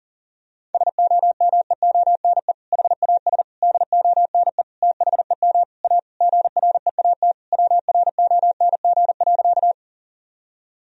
Morse code, heard as a keyed tone, about 37 wpm, 700 Hz. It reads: SOMEONE HAS DONE THEM A GREAT WRONG.
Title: Sentences from Top 1000 Words - Review Part 4 - 40wpm